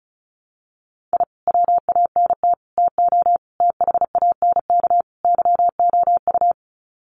Morse code, heard as a keyed tone, about 34 wpm, 700 Hz. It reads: I WANT TO THANK YOU
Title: Sets of 5 Words Encore 35wpm